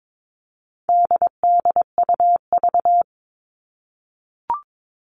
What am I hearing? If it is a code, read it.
Morse code, heard as a keyed tone, about 22 wpm, 700 Hz. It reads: DDUV
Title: All Cut Numbers - 4-Digit 22wpm